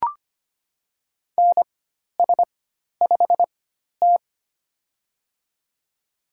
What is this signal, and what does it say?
Morse code, heard as a keyed tone, about 25 wpm, spaced as for 12 wpm, 700 Hz. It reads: NS5T
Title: USA Call Signs - 4x Character Spacing - 25wpm